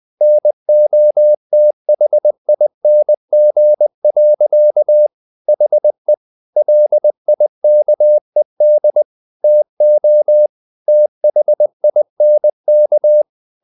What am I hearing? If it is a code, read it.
Morse code, heard as a keyed tone, about 20 wpm, 600 Hz. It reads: NOTHING. HE LIKED TO THINK
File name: the old man and the sea part 2 20wpm